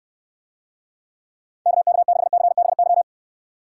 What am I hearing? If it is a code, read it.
Morse code, heard as a keyed tone, about 69 wpm, 700 Hz. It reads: KK6CBY